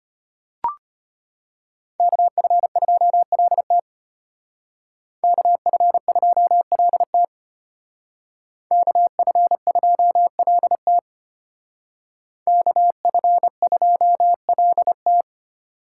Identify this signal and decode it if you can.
Morse code, heard as a keyed tone, about 32 wpm, 700 Hz. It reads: KF2LT KF2LT KF2LT KF2LT